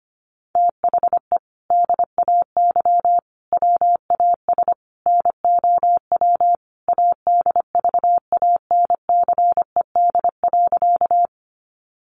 Morse code, heard as a keyed tone, about 25 wpm, 700 Hz. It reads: THE DAY WAS NOW ADVANCED.
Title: Sentences from Top 1200 Words 25wpm